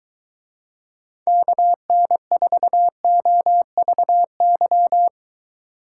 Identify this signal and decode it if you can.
Morse code, heard as a keyed tone, about 23 wpm, 700 Hz. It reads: KN4OVY